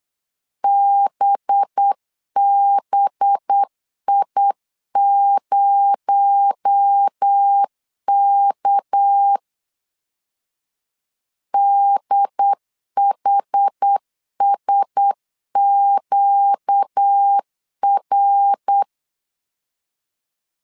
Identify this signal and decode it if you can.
Morse code, heard as a keyed tone, about 8 wpm, 800 Hz. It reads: BBI0K DHSQR
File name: CW / DET / 1 6 1500C S B 8wpm 5142